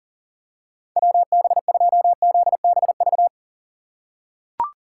Morse code, heard as a keyed tone, about 40 wpm, 700 Hz. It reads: WB2ZBV
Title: USA Call Signs - 6 Characters - 4x Character Spacing - 40wpm